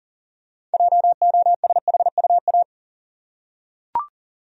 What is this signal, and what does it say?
Morse code, heard as a keyed tone, about 40 wpm, 700 Hz. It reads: JOSHUA
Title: USA Common Names - 40wpm